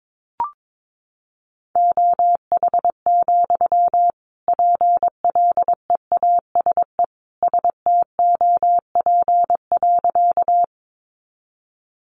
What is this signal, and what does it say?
Morse code, heard as a keyed tone, about 22 wpm, 700 Hz. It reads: OH, PLEASE STOP.
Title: Sentences from Top 1500 Words 22wpm